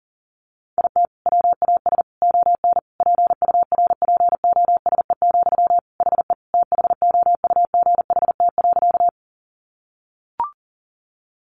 Morse code, heard as a keyed tone, about 40 wpm, 700 Hz. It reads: IT WAS ON PURPOSE, HE THOUGHT.
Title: Sentences from Top 800 Words 40wpm